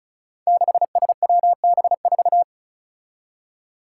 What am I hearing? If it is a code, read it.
Morse code, heard as a keyed tone, about 36 wpm, 700 Hz. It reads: 6SWB4
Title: Five Challenging Letter-Number Combinations 35wpm